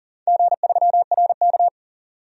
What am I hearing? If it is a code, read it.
Morse code, heard as a keyed tone, about 40 wpm, 700 Hz. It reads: G3RK
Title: DXCC-100 Call Signs - 4 Characters - 40wpm